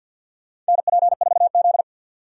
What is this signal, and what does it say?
Morse code, heard as a keyed tone, about 52 wpm, 700 Hz. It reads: NP4Z
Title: USA Call Signs - 4 Characters - 2x Character Spacing - 50wpm